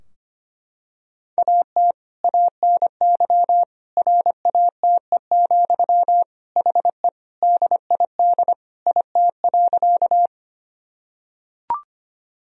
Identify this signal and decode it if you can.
Morse code, heard as a keyed tone, about 25 wpm, 700 Hz. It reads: AT ANY RATE, HE DID IT.
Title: Sentences from Top 1000 Words - Review Part 4 - 25wpm